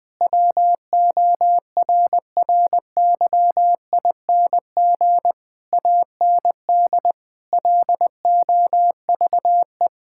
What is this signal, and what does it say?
Morse code, heard as a keyed tone, about 20 wpm, 700 Hz. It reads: WORRYING AND LOVE
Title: Sentences with 8-Word Phrases - 20wpm